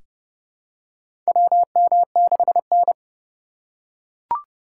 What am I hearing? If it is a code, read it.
Morse code, heard as a keyed tone, about 30 wpm, 700 Hz. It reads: WM6N
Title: USA Call Signs - 2x Character Spacing - 30wpm